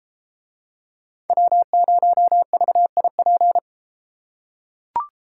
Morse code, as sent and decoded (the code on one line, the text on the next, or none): .-- ----- ...- .. .--.
W0VIP